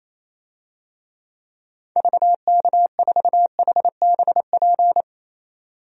Morse code, heard as a keyed tone, about 28 wpm, 700 Hz. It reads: VK4HBP
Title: DXCC-100 Call Signs - 6 Characters - 28wpm